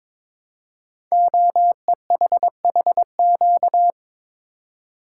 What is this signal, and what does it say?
Morse code, heard as a keyed tone, about 23 wpm, 700 Hz. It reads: OEHHQ
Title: Code Groups - 5 Letters - 22wpm